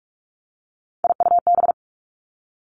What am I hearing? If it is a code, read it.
Morse code, heard as a keyed tone, about 46 wpm, 700 Hz. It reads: IUB
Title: Three Challenging Letter-Number Combinations 45wpm